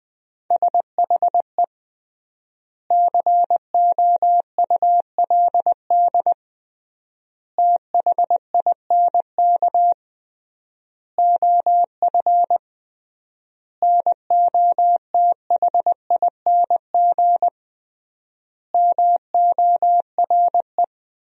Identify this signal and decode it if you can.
Morse code, heard as a keyed tone, about 21 wpm, 700 Hz. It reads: SHE COULD THINK OF NOTHING MORE
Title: 3x Word-Spacing - Sentences from Top 500 Words - 20wpm